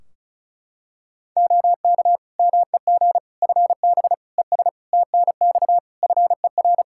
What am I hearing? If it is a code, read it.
Morse code, heard as a keyed tone, about 36 wpm, 700 Hz. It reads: OK MEG FB ES TNX FER
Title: Practice QSO - Exchange 2 - Info - 35wpm